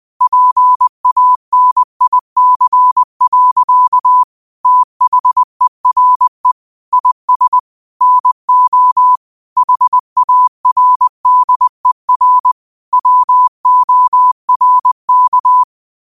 Morse code, heard as a keyed tone, about 20 wpm, 1000 Hz. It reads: PANIC. THERE IS NO HARDER WORK
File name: qotdfiles / 20wpm